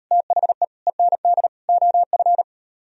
Morse code, heard as a keyed tone, about 35 wpm, 700 Hz. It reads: THE END OF